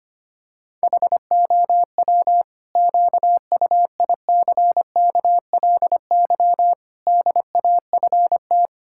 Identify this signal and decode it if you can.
Morse code, heard as a keyed tone, about 25 wpm, 700 Hz. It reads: HOW QUICKLY DAFT
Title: Pangram Challenge v2 - 25wpm